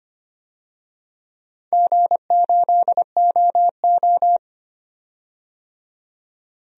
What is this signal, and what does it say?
Morse code, heard as a keyed tone, about 25 wpm, 700 Hz. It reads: G8OO
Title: DXCC-100 Call Signs - 4 Characters - 25wpm